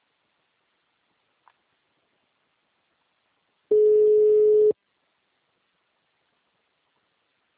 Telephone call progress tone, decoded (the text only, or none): ringback tone